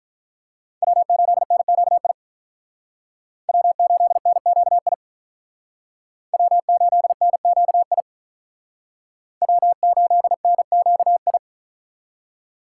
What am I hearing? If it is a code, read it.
Morse code, heard as a keyed tone, about 43 wpm, 700 Hz. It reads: W8NQI W8NQI W8NQI W8NQI
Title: Speed Racing - USA 1x3 Call Signs 35wpm